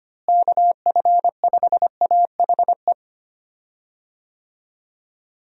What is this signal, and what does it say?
Morse code, heard as a keyed tone, about 25 wpm, 700 Hz. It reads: KF5AHE
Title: USA Call Signs - 6 Characters - Repeated 2x - 25wpm